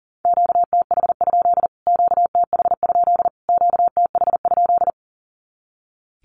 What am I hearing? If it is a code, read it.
Morse code, heard as a keyed tone, about 40 wpm, 700 Hz. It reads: QTH? QTH? QTH?
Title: Q-Codes Repeated 3x 40wpm